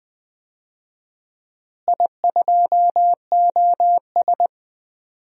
Morse code, heard as a keyed tone, about 21 wpm, 700 Hz. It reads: I2OS
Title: DXCC-100 Call Signs - 4 Characters - Repeated 3x - 20wpm